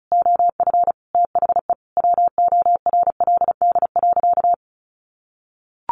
Morse code, heard as a keyed tone, about 34 wpm, 700 Hz. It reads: OF THE WORLD.
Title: Sentences from Top 1100 Words 35wpm